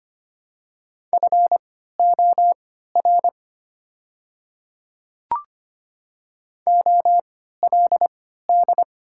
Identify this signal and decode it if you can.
Morse code, heard as a keyed tone, about 25 wpm, spaced as for 14 wpm, 690 Hz. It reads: FOR OLD
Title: 3x Character Spacing - Top 200 Words - 25wpm